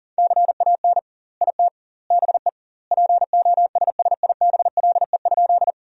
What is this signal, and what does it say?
Morse code, heard as a keyed tone, about 37 wpm, 700 Hz. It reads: CAN IT BE POSSIBLE?
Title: 2x Word-Spacing - Sentences from Top 500 Words - 40wpm